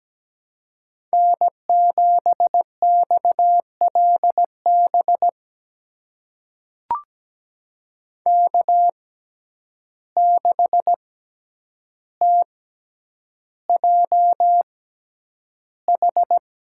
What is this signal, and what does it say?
Morse code, heard as a keyed tone, about 17 wpm, spaced as for 9 wpm, 700 Hz. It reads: N7XLB K6TJH
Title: USA Call Signs - 5 Characters - 6x Character Spacing - 17wpm